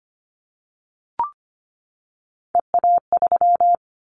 Morse code, heard as a keyed tone, about 25 wpm, 700 Hz. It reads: EA3
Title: Callsign Prefixes for Top 25 DXCC - 25wpm